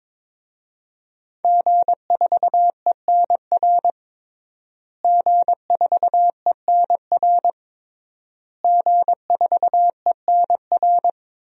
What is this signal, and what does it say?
Morse code, heard as a keyed tone, about 22 wpm, 700 Hz. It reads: G4ENR G4ENR G4ENR
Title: DXCC-100 Call Signs - 5 Characters - Repeated 3x - 22wpm